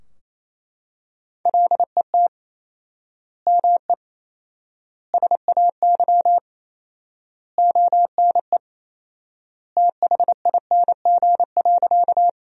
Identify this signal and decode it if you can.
Morse code, heard as a keyed tone, about 28 wpm, 700 Hz. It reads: LET ME SAY ONE THING.
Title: 4x Word-Spacing - Sentences from Top 300 Words - 28wpm